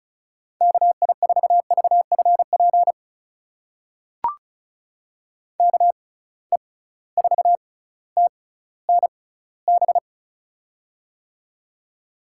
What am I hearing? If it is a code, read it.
Morse code, heard as a keyed tone, about 36 wpm, spaced as for 18 wpm, 700 Hz. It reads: KI4VFP KE4TNB
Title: USA Call Signs - 6 Characters - 6x Character Spacing - 35wpm